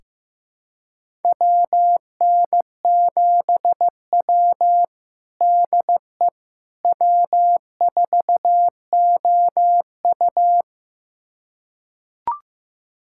Morse code, heard as a keyed tone, about 16 wpm, 700 Hz. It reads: WN7W DE W4OU
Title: USA Callsign DE Callsign - 4 Characters - 15wpm